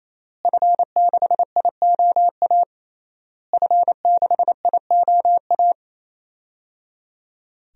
Morse code, heard as a keyed tone, about 28 wpm, 700 Hz. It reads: F6IOA F6IOA
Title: DXCC-100 Call Signs - 5 Characters - Repeated 3x - 28wpm